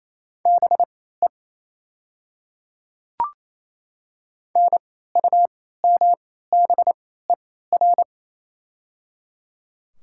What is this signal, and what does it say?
Morse code, heard as a keyed tone, about 28 wpm, spaced as for 16 wpm, 690 Hz. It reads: BE NUMBER
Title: 3x Character Spacing - Top 100 Words - 28wpm